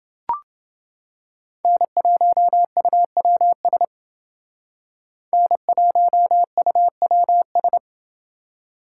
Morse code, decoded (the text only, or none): N1UWS N1UWS